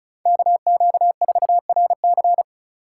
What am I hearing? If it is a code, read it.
Morse code, heard as a keyed tone, about 36 wpm, 700 Hz. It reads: KQ4RC
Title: USA Call Signs - 5 Characters - 3x Character Spacing - 35wpm